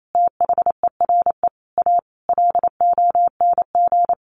Morse code, heard as a keyed tone, about 28 wpm, 700 Hz. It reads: THERE A LONG
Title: Sentences from Top 400 Words 28wpm